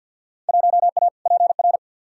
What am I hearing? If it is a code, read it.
Morse code, heard as a keyed tone, about 46 wpm, 700 Hz. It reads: 1A PR
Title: Field Day Exchanges Encore - 50wpm